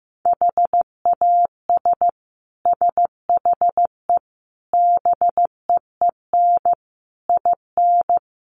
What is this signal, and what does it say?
Morse code, heard as a keyed tone, about 14 wpm, 700 Hz. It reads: HAS SHE BEEN IN
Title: Sentences from Top 700 Words 15wpm